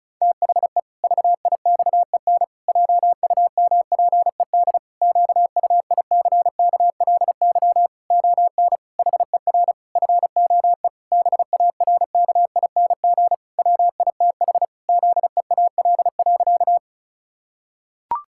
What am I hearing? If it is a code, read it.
Morse code, heard as a keyed tone, about 34 wpm, 700 Hz. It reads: THE VIXEN JUMPED QUICKLY ON HER FOE BARKING WITH ZEAL.
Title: Pangram Challenge v2 - 35wpm